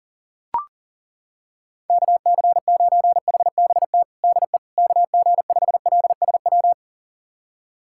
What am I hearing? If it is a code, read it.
Morse code, heard as a keyed tone, about 40 wpm, 700 Hz. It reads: KC9HBT DE KG5LSW
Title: USA Callsign DE Callsign 40wpm